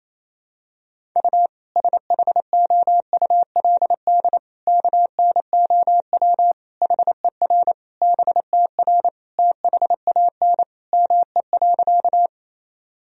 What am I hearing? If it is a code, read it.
Morse code, heard as a keyed tone, about 28 wpm, 700 Hz. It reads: U SHOULD KNOW HER BTR THAN ME.
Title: Sentences from Top 400 Words and Abbreviations 28wpm